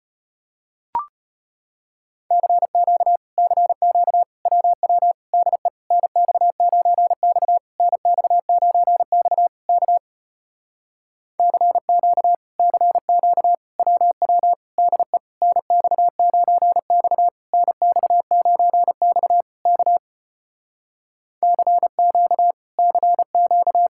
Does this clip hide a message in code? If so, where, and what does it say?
Yes: Morse code from 2.3 to 23.9 s, heard as a keyed tone, about 33 wpm, 700 Hz. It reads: CQ CQ WW DE NX9X NX9X K CQ CQ WW DE NX9X NX9X K CQ CQ